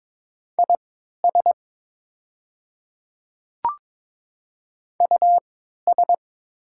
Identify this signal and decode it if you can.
Morse code, heard as a keyed tone, about 22 wpm, spaced as for 13 wpm, 700 Hz. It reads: IS US
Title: 3x Character Spacing - Top 200 Words - 22wpm